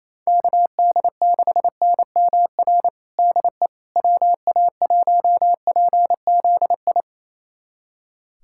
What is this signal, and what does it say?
Morse code, heard as a keyed tone, about 28 wpm, 700 Hz. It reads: KD6NMR DE WA1PZI
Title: USA Callsign DE Callsign 28wpm